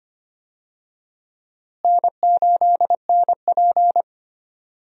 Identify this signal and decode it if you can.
Morse code, heard as a keyed tone, about 25 wpm, 700 Hz. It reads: N8NP
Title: USA Call Signs - 4 Characters - Repeated 2x - 25wpm